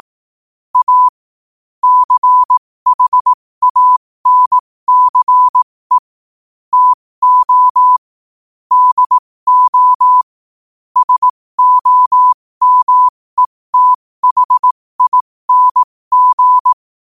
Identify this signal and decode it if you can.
Morse code, heard as a keyed tone, about 18 wpm, spaced as for 15 wpm, 1000 Hz. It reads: A CHANCE TO DO SOMETHING
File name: 15wpm